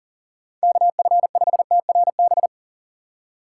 Morse code, heard as a keyed tone, about 39 wpm, 700 Hz. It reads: KF5TRB
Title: USA Call Signs - 6 Characters - 3x Character Spacing - 40wpm